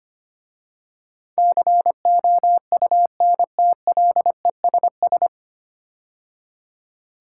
Morse code, heard as a keyed tone, about 25 wpm, 700 Hz. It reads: COUNTLESS
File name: practice / 25wpm / 9 Letter Words - First 3 Letters Given - 25wpm